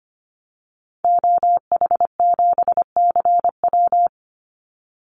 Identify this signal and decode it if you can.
Morse code, heard as a keyed tone, about 25 wpm, 700 Hz. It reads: OH7CW
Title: Call Signs Repeated 3x - 25wpm